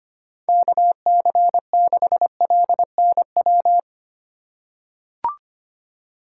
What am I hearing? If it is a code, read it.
Morse code, heard as a keyed tone, about 25 wpm, 700 Hz. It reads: KC6LNW
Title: USA Call Signs - 6 Characters - 2x Character Spacing - 25wpm